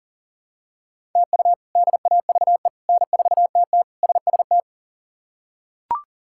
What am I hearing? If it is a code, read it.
Morse code, heard as a keyed tone, about 40 wpm, 700 Hz. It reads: TU DAVE N4TT SST
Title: K1USN SST QSO Practice - 40wpm